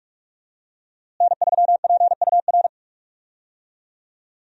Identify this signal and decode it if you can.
Morse code, heard as a keyed tone, about 42 wpm, 700 Hz. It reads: N3PUR
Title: USA Call Signs - 5 Characters - Repeated 3x - 45wpm